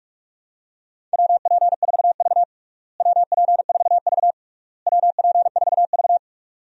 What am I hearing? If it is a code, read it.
Morse code, heard as a keyed tone, about 46 wpm, 700 Hz. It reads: WP4V WP4V WP4V
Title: USA Call Signs - 4 Characters - Repeated 3x - 45wpm